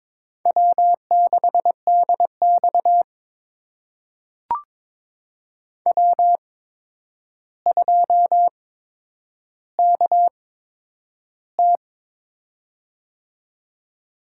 Morse code, heard as a keyed tone, about 23 wpm, spaced as for 10 wpm, 700 Hz. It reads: W6DX W2KT
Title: USA Call Signs - 4 Characters - 8x Character Spacing - 22wpm